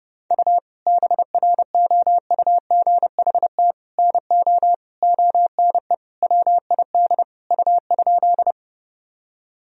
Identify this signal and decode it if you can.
Morse code, heard as a keyed tone, about 32 wpm, 700 Hz. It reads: U BROUGHT NO ONE WID U?
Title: Sentences from Top 500 Words and Abbreviations 30wpm